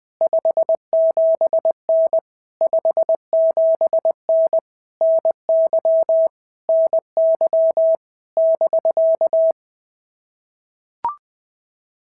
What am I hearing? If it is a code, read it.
Morse code, heard as a keyed tone, about 19 wpm, 650 Hz. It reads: 57N 57N NY NY <BK>
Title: POTA QSO Practice - 20wpm